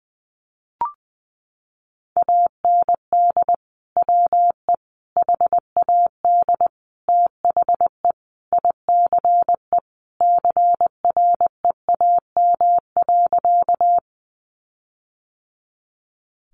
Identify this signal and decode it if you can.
Morse code, heard as a keyed tone, about 20 wpm, 700 Hz. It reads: AND WE HAD THE ICE CREAM.
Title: Sentences from Top 1500 Words 20wpm